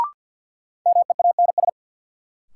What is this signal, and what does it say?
Morse code, heard as a keyed tone, about 52 wpm, 700 Hz. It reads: MEANS
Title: Sets of 5 Words - New Word Drill - 50wpm